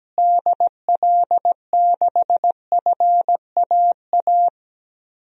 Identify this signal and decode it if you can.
Morse code, heard as a keyed tone, about 17 wpm, 700 Hz. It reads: DL6FAA
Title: DXCC-100 Call Signs - 6 Characters - Repeated 3x - 17wpm